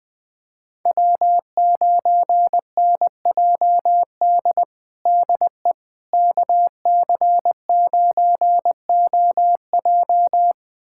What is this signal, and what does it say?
Morse code, heard as a keyed tone, about 20 wpm, 700 Hz. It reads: W9NJD DE KC9OJ
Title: USA Callsign DE Callsign - 5 Characters - 20wpm